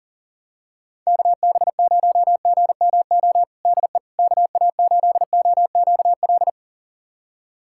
Morse code, heard as a keyed tone, about 40 wpm, 700 Hz. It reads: KB0GMO DE KA8OQL
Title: USA Callsign DE Callsign 40wpm